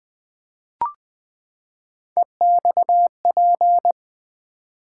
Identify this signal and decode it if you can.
Morse code, heard as a keyed tone, about 20 wpm, 700 Hz. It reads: EXP
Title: Build A Word - 3 Letters At A Time - 9-Letter Words - 20wpm